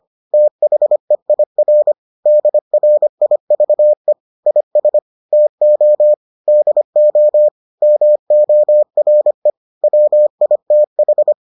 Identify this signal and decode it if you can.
Morse code, heard as a keyed tone, about 25 wpm, 600 Hz. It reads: THEIR DRIVE IS TO DO MORE WITH